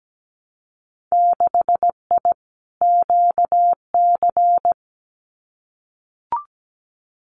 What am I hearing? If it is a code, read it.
Morse code, heard as a keyed tone, about 17 wpm, 700 Hz. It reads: 6I QC
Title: Winter Field Day Exchanges Encore 17wpm